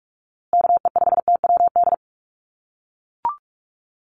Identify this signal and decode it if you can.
Morse code, heard as a keyed tone, about 44 wpm, 700 Hz. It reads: KE5TWD
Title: USA Call Signs Repeated 3x 45wpm